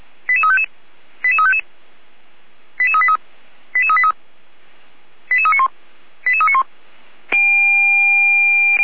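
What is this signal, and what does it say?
Five-tone selective call (ZVEI), heard as quick runs of five tones, tones of about 70 ms. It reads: 88355 88355 88383 88383 88391 88391
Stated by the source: ZVEI five-tone selective call 88327 received on an SDR.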